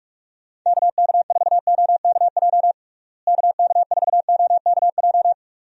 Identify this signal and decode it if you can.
Morse code, heard as a keyed tone, about 42 wpm, 700 Hz. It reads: KK4OKJ KK4OKJ
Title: USA Call Signs - 6 Characters - Repeated 2x - 45wpm